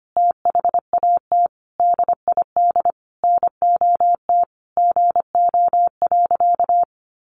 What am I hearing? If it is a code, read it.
Morse code, heard as a keyed tone, about 25 wpm, 700 Hz. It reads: THAT DID NOT GO.
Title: Sentences from Top 800 Words 25wpm